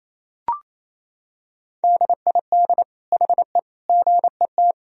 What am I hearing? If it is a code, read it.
Morse code, heard as a keyed tone, about 28 wpm, 700 Hz. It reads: DID HE GET
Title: Sentences from Top 1000 Words - Review Part 4 - 28wpm